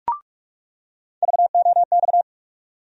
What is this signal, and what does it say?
Morse code, heard as a keyed tone, about 43 wpm, 700 Hz. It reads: VOX
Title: USA Call Sign Postfixes - 3 Letters 45wpm